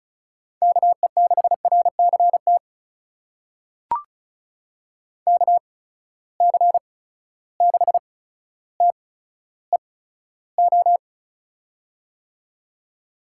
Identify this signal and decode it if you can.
Morse code, heard as a keyed tone, about 36 wpm, spaced as for 15 wpm, 700 Hz. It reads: KE6RCT KC6TEO
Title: USA Call Signs - 6 Characters - 8x Character Spacing - 35wpm